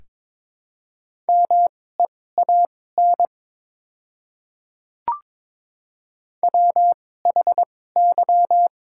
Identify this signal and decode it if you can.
Morse code, heard as a keyed tone, about 22 wpm, spaced as for 16 wpm, 700 Hz. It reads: MEAN WHY
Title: 2x Character Spacing - Top 300 Words - 22wpm